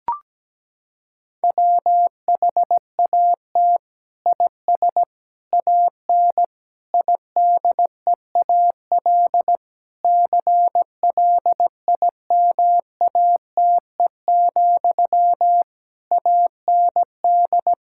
Morse code, heard as a keyed tone, about 17 wpm, 700 Hz. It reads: WHAT IS AN IDEAL CLIMATE, AND
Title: Sentences from Top 1600 Words 17wpm